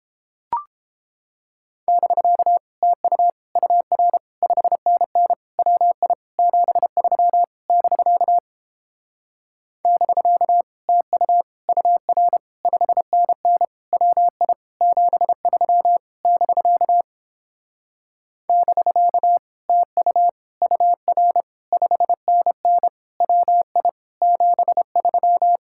Morse code, heard as a keyed tone, about 29 wpm, 700 Hz. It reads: <BK> TU UR 5NN WI 73 <BK> <BK> TU UR 5NN WI 73 <BK> <BK> TU UR 5NN WI 73